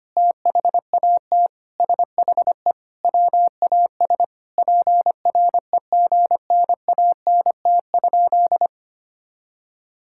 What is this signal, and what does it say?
Morse code, heard as a keyed tone, about 25 wpm, 700 Hz. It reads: THAT SHE WAS PREGNANT?
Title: Sentences from Top 1000 Words - Review Part 6 - 25wpm